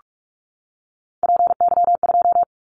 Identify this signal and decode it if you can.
Morse code, heard as a keyed tone, about 44 wpm, 700 Hz. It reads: PY2
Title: Callsign Prefixes for Top 100 DXCC - 45wpm